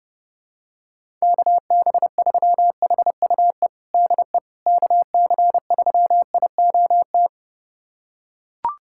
Morse code, heard as a keyed tone, about 30 wpm, 700 Hz. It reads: KB3HUE DE KC3IOT
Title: USA Callsign DE Callsign - 6 Characters - 30wpm